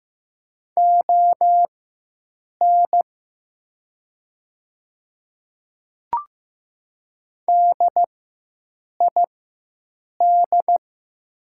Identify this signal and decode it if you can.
Morse code, heard as a keyed tone, about 15 wpm, spaced as for 7 wpm, 700 Hz. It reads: ON DID
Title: 4x Character Spacing - Top 100 Words - 15wpm